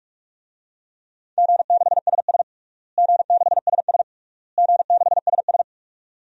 Morse code, heard as a keyed tone, about 46 wpm, 700 Hz. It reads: G6SS G6SS G6SS
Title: DXCC-100 Call Signs - 4 Characters - Repeated 3x - 45wpm